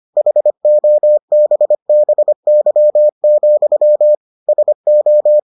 Morse code, heard as a keyed tone, about 25 wpm, 600 Hz. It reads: HOBBY, SO